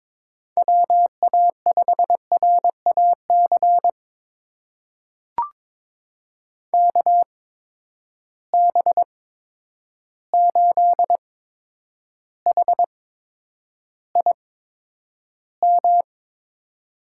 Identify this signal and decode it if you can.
Morse code, heard as a keyed tone, about 22 wpm, spaced as for 9 wpm, 700 Hz. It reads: WA5RAC KB8HIM
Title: USA Call Signs - 6 Characters - 8x Character Spacing - 22wpm